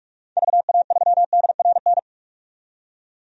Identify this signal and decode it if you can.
Morse code, heard as a keyed tone, about 46 wpm, 700 Hz. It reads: VA3DRN
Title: DXCC-100 Call Signs - 6 Characters - 45wpm